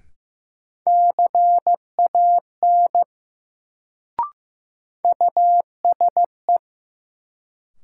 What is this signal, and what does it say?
Morse code, heard as a keyed tone, about 15 wpm, 700 Hz. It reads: CAN USE
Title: Sentences from Top 100 Words - New Word Drill - 15wpm